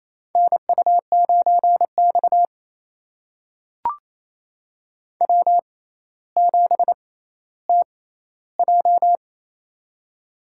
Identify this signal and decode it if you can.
Morse code, heard as a keyed tone, about 28 wpm, spaced as for 14 wpm, 700 Hz. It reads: NU9X W7TJ
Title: USA Call Signs - 4 Characters - 6x Character Spacing - 28wpm